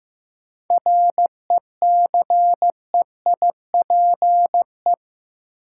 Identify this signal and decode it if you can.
Morse code, heard as a keyed tone, about 15 wpm, 700 Hz. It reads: RECEIPE
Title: Sentences from Top 1200 Words - New Word Drill Alt - 15wpm